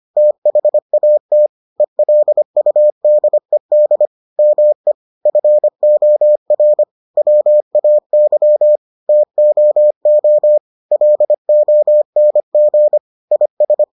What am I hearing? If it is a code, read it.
Morse code, heard as a keyed tone, about 25 wpm, 600 Hz. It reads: THAT ELUDED ME FOR WAY TOO LONG IS